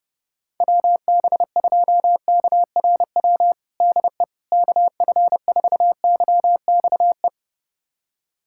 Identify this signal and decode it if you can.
Morse code, heard as a keyed tone, about 32 wpm, 700 Hz. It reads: WB2KRW DE KF4YXE
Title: USA Callsign DE Callsign - 6 Characters - 30wpm